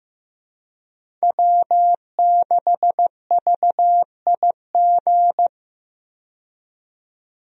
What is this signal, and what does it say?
Morse code, heard as a keyed tone, about 15 wpm, 700 Hz. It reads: W6VIG
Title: USA Call Signs - 5 Characters - 15wpm